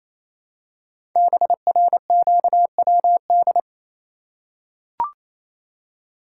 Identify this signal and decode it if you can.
Morse code, heard as a keyed tone, about 27 wpm, 700 Hz. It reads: BRQWD
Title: Code Groups - 5 Letters - 28wpm